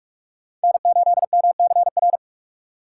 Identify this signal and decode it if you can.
Morse code, heard as a keyed tone, about 46 wpm, 690 Hz. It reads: N8MCR
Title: USA Call Signs - 5 Characters - 4x Character Spacing - 45wpm